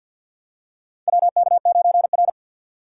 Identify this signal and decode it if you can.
Morse code, heard as a keyed tone, about 52 wpm, 690 Hz. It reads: WK9R
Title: USA Call Signs - 4 Characters - 6x Character Spacing - 50wpm